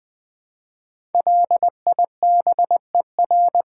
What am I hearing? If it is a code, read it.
Morse code, heard as a keyed tone, about 20 wpm, 700 Hz. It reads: LIBER
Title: Build A Word - 5 Letters At A Time - 10-Letter Words - 20wpm